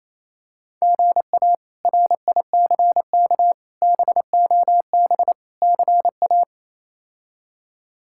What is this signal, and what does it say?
Morse code, heard as a keyed tone, about 28 wpm, 700 Hz. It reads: GA RICK BOB CA
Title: K1USN SST QSO Practice - 28wpm